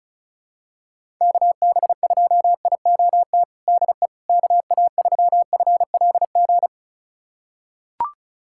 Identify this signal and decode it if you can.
Morse code, heard as a keyed tone, about 36 wpm, 700 Hz. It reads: KB2IOT DE KA3FLG
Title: USA Callsign DE Callsign - 6 Characters - 35wpm